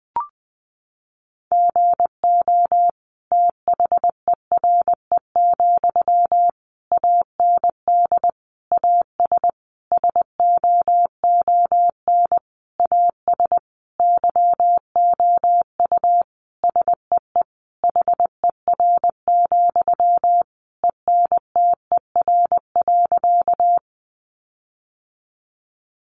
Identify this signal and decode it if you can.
Morse code, heard as a keyed tone, about 20 wpm, 700 Hz. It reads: GO THERE, AND AS SOON AS YOU SEE HER, ENTER.
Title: Sentences from Top 1300 Words 20wpm